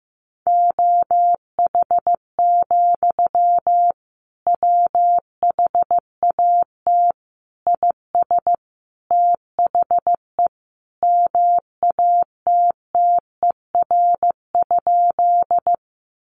Morse code, heard as a keyed tone, about 15 wpm, 700 Hz. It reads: OH, WHAT IS THE MATTER?
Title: Sentences from Top 700 Words 15wpm